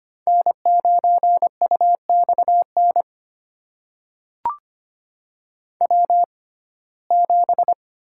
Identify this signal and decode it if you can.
Morse code, heard as a keyed tone, about 25 wpm, spaced as for 18 wpm, 700 Hz. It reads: N9UXN W7
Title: USA Call Signs - 5 Characters - 6x Character Spacing - 25wpm